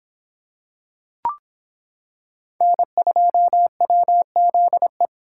Morse code, heard as a keyed tone, about 25 wpm, 700 Hz. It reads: N2WZE